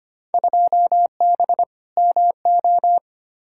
Morse code, heard as a keyed tone, about 25 wpm, 700 Hz. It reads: 2B MO
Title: Field Day Exchanges Encore - 25wpm